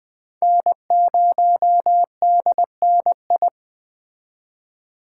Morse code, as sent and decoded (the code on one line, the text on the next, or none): -. ----- -.. -. ..
N0DNI